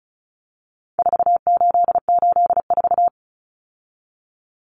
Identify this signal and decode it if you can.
Morse code, heard as a keyed tone, about 34 wpm, 700 Hz. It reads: 4884
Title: Common Cut Numbers - 4-Digit 35wpm